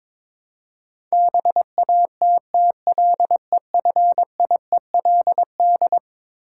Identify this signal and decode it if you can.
Morse code, heard as a keyed tone, about 21 wpm, 700 Hz. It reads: BATTLEFIELD
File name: practice / 22wpm / Eleven Letter Words - 22wpm